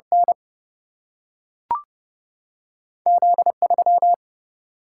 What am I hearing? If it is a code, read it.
Morse code, heard as a keyed tone, about 30 wpm, 700 Hz. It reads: N Z3
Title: Single-Letter Single-Number 30wpm